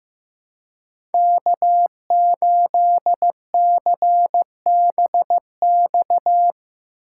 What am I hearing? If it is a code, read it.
Morse code, heard as a keyed tone, about 15 wpm, 700 Hz. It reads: K8CBX
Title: USA Call Signs - 5 Characters - 4x Character Spacing - 15wpm